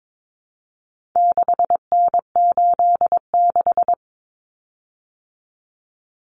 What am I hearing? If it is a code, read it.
Morse code, heard as a keyed tone, about 22 wpm, 700 Hz. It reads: 6N86
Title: Common Cut Numbers - 4-Digit 22wpm